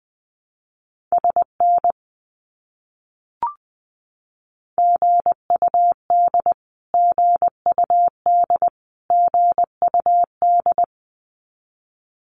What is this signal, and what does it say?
Morse code, heard as a keyed tone, about 20 wpm, 700 Hz. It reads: SN GUD GUD GUD
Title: Abbreviations Repeated 3x - 20wpm